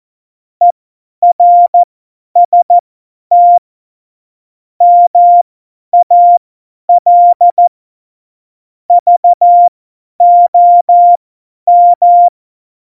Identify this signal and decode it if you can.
Morse code, heard as a keyed tone, about 14 wpm, spaced as for 10 wpm, 700 Hz. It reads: ERST MAL VOM
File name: Morsekurs / Witze / Witz 08. Januar 2019, 50 F70